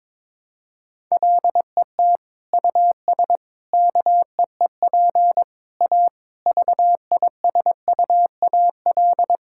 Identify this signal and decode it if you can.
Morse code, heard as a keyed tone, about 21 wpm, 700 Hz. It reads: LET US KEEP A VISUAL
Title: Sentences from Top 1000 Words - Review Part 6 - 22wpm